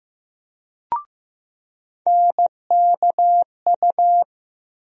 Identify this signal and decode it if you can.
Morse code, heard as a keyed tone, about 15 wpm, 700 Hz. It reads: NKU
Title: USA Call Sign Postfixes - 3 Letters 15wpm